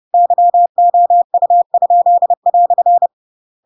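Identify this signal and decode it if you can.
Morse code, heard as a keyed tone, about 30 wpm, 700 Hz. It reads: YOU?"
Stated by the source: Chapter 2 of The Adventures of Pinocchio read as a book in Morse code (CW) at 30 wpm, by ik1hge.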